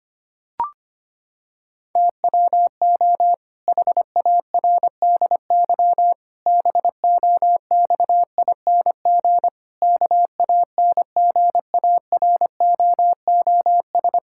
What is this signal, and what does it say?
Morse code, heard as a keyed tone, about 25 wpm, 700 Hz. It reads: TWO HARDY BOXING KANGAROOS
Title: Pangram Challenge v2 - 25wpm